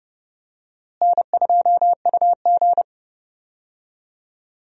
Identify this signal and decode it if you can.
Morse code, heard as a keyed tone, about 30 wpm, 700 Hz. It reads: N2UG
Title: USA Call Signs - 4 Characters - Repeated 3x - 30wpm